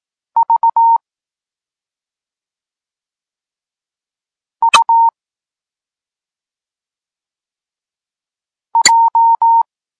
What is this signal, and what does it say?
Morse code, heard as a keyed tone, about 18 wpm, 920 Hz. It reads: V U J